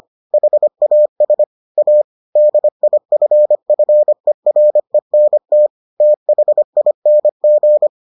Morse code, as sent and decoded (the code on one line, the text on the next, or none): .... .- ... / .- / -.. .. ..-. ..-. .-. -. - / - .... .. -. --.
HAS A DIFFERENT THING